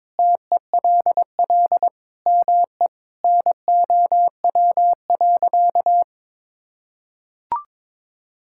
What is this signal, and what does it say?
Morse code, heard as a keyed tone, about 22 wpm, 700 Hz. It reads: TELL ME NOW.
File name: practice / 22wpm / Sentences with 4-Word Phrases - 22wpm